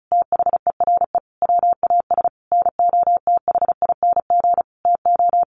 Morse code, heard as a keyed tone, about 34 wpm, 700 Hz. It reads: THERE WAS NOTHING TO
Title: Sentences from Top 1000 Words 35wpm